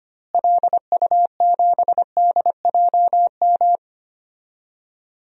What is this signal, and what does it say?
Morse code, heard as a keyed tone, about 25 wpm, 700 Hz. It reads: LU7DJM
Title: DXCC-100 Call Signs - 6 Characters - Repeated 2x - 25wpm